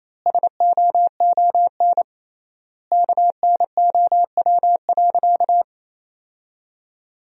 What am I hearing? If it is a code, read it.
Morse code, heard as a keyed tone, about 28 wpm, 700 Hz. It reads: SOON KNOW.
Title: 3x Word-Spacing - Sentences from Top 400 Words - 28wpm